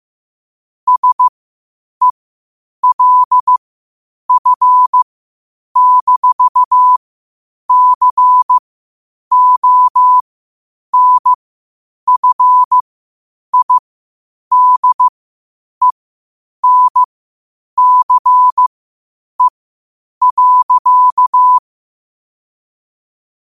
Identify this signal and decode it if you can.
Morse code, heard as a keyed tone, about 15 wpm, spaced as for 8 wpm, 1000 Hz. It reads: SELF-CONFIDENCE.
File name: qotdfiles / 8wpm